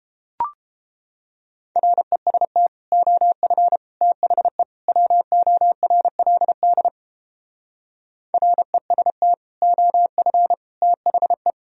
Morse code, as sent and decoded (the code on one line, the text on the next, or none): .-. ... - / --- ..-. / - .... / .-- --- .-. .-.. -.. / .-. ... - / --- ..-. / - .... .
REST OF THE WORLD REST OF THE